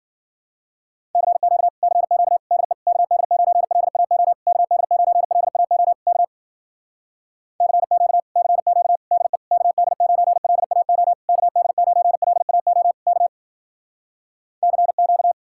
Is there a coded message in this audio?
Yes: Morse code from 1.2 to 15.3 s, heard as a keyed tone, about 55 wpm, 700 Hz. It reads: CQ CQ DE KD9LAO KD9LAO K CQ CQ DE KD9LAO KD9LAO K CQ